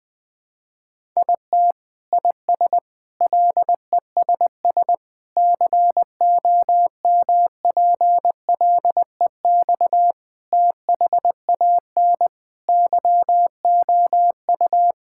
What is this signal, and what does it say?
Morse code, heard as a keyed tone, about 20 wpm, 700 Hz. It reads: IT IS LESS COMPLEX THAN YOU
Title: Sentences from Top 1000 Words - Review Part 6 - 20wpm